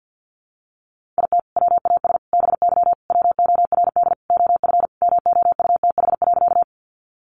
Morse code, heard as a keyed tone, about 50 wpm, 700 Hz. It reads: IT WAS BY WORD OF MOUTH.
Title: Sentences from Top 1500 Words 50wpm